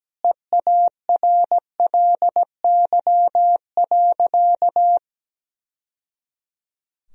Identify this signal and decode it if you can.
Morse code, heard as a keyed tone, about 17 wpm, 700 Hz. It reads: EARLY.
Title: Sentences from Top 1000 Words - Review Part 2 - 17wpm